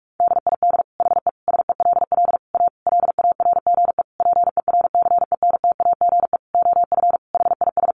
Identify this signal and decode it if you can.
Morse code, heard as a keyed tone, about 44 wpm, 700 Hz. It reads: DID HE SELL A LARGE PERCENTAGE OF HIS
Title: Sentences from Top 1300 Words 45wpm